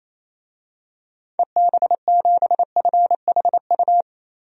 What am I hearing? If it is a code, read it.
Morse code, heard as a keyed tone, about 28 wpm, 700 Hz. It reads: EB7FHU
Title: DXCC-100 Call Signs - 6 Characters - 28wpm